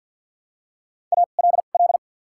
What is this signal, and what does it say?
Morse code, heard as a keyed tone, about 54 wpm, spaced as for 37 wpm, 700 Hz. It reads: ALL
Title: 2x Character Spacing - Top 300 Words - 50wpm